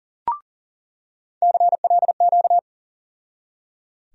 Morse code, heard as a keyed tone, about 37 wpm, 700 Hz. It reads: CLQ
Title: USA Call Sign Postfixes - 3 Letters 40wpm